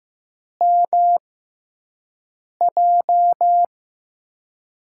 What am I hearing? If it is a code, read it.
Morse code, heard as a keyed tone, about 15 wpm, spaced as for 5 wpm, 700 Hz. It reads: MJ
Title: USA Call Signs - 6x Character Spacing - 15wpm